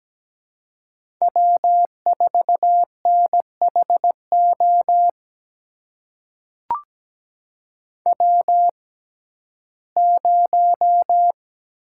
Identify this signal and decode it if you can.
Morse code, heard as a keyed tone, about 17 wpm, spaced as for 12 wpm, 700 Hz. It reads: W4NHO W0
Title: USA Call Signs - 5 Characters - 6x Character Spacing - 17wpm